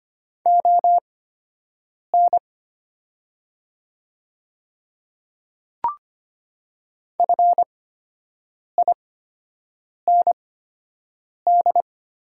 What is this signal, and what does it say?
Morse code, heard as a keyed tone, about 25 wpm, spaced as for 7 wpm, 700 Hz. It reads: ON FIND
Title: 8x Character Spacing - Top 100 Words - 25wpm